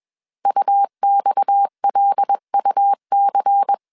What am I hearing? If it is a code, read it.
Morse code, heard as a keyed tone, about 21 wpm, 800 Hz. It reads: UXLUC
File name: CW / DET / 1 6 1500C S C 20wpm 2116